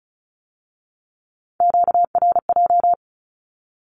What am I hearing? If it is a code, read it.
Morse code, heard as a keyed tone, about 36 wpm, 700 Hz. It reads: QRJ
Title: Q-Codes with Spoken Meaning - 35wpm